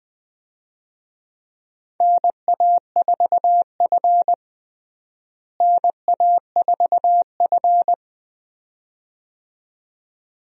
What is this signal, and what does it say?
Morse code, heard as a keyed tone, about 20 wpm, 700 Hz. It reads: NA4F NA4F
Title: USA Call Signs - 4 Characters - Repeated 2x - 20wpm